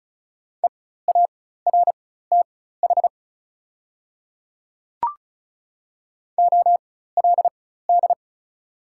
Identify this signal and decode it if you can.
Morse code, heard as a keyed tone, about 36 wpm, spaced as for 16 wpm, 700 Hz. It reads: EARTH OLD
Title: 4x Character Spacing - Top 200 Words - 35wpm